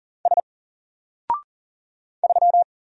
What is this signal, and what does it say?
Morse code, heard as a keyed tone, about 40 wpm, 700 Hz. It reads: S 3
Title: Single Letter-Number Alt 40wpm